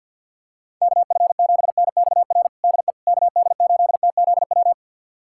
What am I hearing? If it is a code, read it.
Morse code, heard as a keyed tone, about 51 wpm, 700 Hz. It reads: KF7NXR DE KD8TZW
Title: USA Callsign DE Callsign - 6 Characters - 50wpm